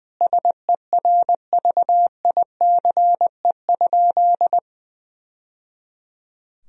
Morse code, heard as a keyed tone, about 20 wpm, 700 Hz. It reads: SERVICE?
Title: Sentences from Top 1000 Words - Review Part 4 - 20wpm